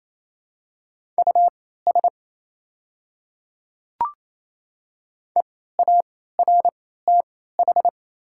Morse code, heard as a keyed tone, about 28 wpm, spaced as for 16 wpm, 690 Hz. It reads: US EARTH
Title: 3x Character Spacing - Top 200 Words - 28wpm